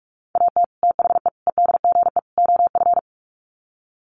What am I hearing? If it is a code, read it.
Morse code, heard as a keyed tone, about 44 wpm, 700 Hz. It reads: AT THE EDGE OF
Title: Sets of 4 Words 45wpm